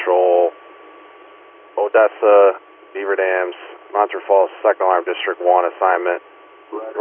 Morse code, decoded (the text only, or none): none